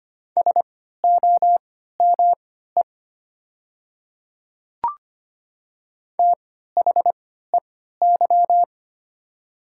Morse code, heard as a keyed tone, about 25 wpm, spaced as for 14 wpm, 700 Hz. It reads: SOME THEY
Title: 3x Character Spacing - Top 100 Words - 25wpm